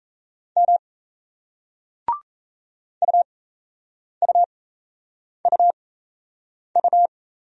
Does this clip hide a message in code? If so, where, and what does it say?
Yes: Morse code from 0.6 to 7.1 s, heard as a keyed tone, about 37 wpm, spaced as for 8 wpm, 700 Hz. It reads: M UUUU